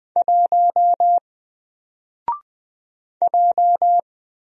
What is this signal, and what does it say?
Morse code, heard as a keyed tone, about 20 wpm, 700 Hz. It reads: 1 J
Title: Single Character (Letter:Number:Punctuation:Prosign) 20wpm